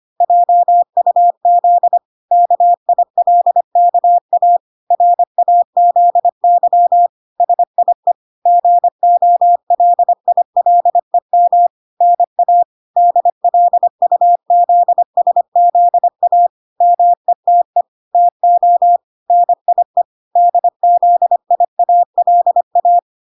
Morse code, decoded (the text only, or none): JUZ KILKA RAZY SIE GOLILEM NA DLUZSZA METE TO NIE DZIALA